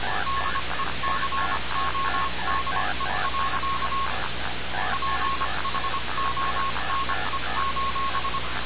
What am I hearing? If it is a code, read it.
Morse code, heard as a keyed tone, about 25 wpm, spaced as for 22 wpm, 1090 Hz, with radio noise. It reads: DECODE. PHONE:
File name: radiocap / REC z